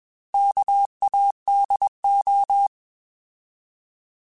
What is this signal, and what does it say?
Morse code, heard as a keyed tone, about 21 wpm, 800 Hz. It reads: KADO